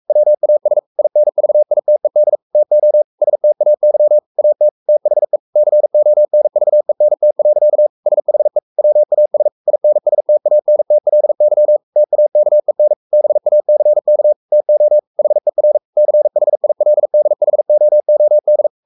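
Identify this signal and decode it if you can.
Morse code, heard as a keyed tone, about 44 wpm, 600 Hz. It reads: WAS INVITED TO STAY AT THE CONVENT. SHE WAS INSTANTLY TAKEN BACK TO HER CHILDHOOD
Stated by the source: Morse code at 43 wpm sending Ken Follett's novel Never in English, from CW med Gnister.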